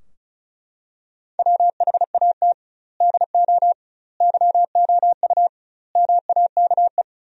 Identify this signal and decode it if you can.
Morse code, heard as a keyed tone, about 36 wpm, 700 Hz. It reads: WHAT DO YOU MAKE
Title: 2x Word-Spacing - Sentences from Top 100 Words - 35wpm